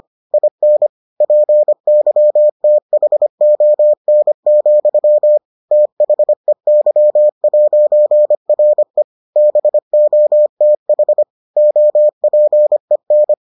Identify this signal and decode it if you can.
Morse code, heard as a keyed tone, about 25 wpm, 600 Hz. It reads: IN PYTHON, THEY'RE BOTH OPEN